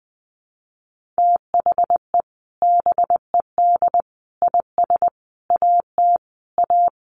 Morse code, heard as a keyed tone, about 20 wpm, 700 Hz. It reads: THE BED IS AT A
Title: Sentences from Top 1500 Words 20wpm